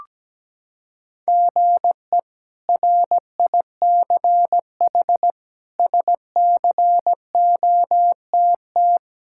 Morse code, heard as a keyed tone, about 17 wpm, 700 Hz. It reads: GE RICH SCOTT
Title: K1USN SST QSO Practice - 17wpm